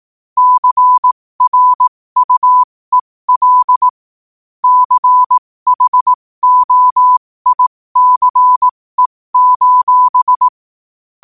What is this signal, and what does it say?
Morse code, heard as a keyed tone, about 18 wpm, spaced as for 15 wpm, 1000 Hz. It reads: CRUEL CHOICE:
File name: qotdfiles / 15wpm